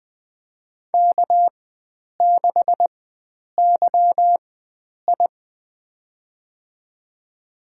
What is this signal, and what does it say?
Morse code, heard as a keyed tone, about 20 wpm, spaced as for 9 wpm, 700 Hz. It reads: K6YI
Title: USA Call Signs - 4 Characters - 4x Character Spacing - 20wpm